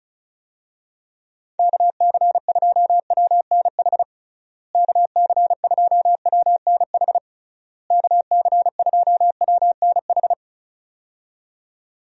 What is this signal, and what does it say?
Morse code, heard as a keyed tone, about 36 wpm, 700 Hz. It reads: KC2WNH KC2WNH KC2WNH
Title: USA Call Signs - 6 Characters - Repeated 3x - 35wpm